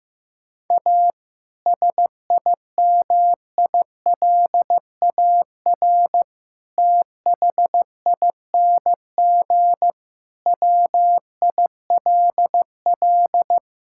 Morse code, heard as a keyed tone, about 15 wpm, 700 Hz. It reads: A SIMILAR THING WILL
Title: Sentences from Top 1000 Words - Review Part 5 - 15wpm